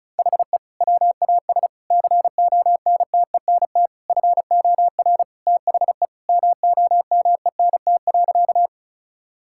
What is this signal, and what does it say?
Morse code, heard as a keyed tone, about 34 wpm, 700 Hz. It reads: HE WAS CONTENT FOR THE MOMENT.
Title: Sentences from Top 1000 Words - Review Part 6 - 35wpm